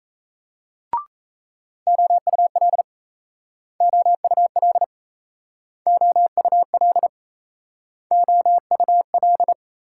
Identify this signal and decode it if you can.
Morse code, heard as a keyed tone, about 36 wpm, 700 Hz. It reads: OUL OUL OUL OUL